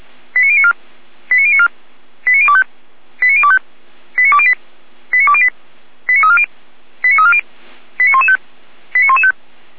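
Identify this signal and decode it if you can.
Five-tone selective call (ZVEI), heard as quick runs of five tones, tones of about 70 ms. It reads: 89094 89094 89025 89025 89208 89208 89340 89340 89115 89115
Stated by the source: Five-tone ZVEI selective call 89092, received by SDR.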